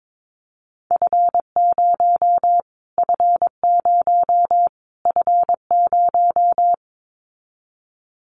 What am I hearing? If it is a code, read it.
Morse code, heard as a keyed tone, about 22 wpm, 700 Hz. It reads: F0 F0 F0
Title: Callsign Prefixes for Top 25 DXCC - Repeated 3x - 22wpm